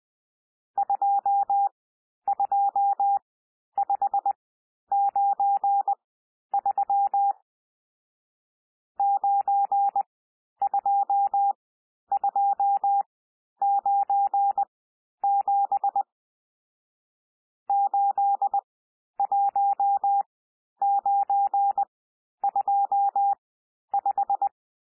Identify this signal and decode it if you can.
Morse code, heard as a keyed tone, about 20 wpm, spaced as for 11 wpm, 800 Hz. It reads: 22593 92297 81925
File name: cwmp3 / N1 SOUND NUMBERS 20 wpm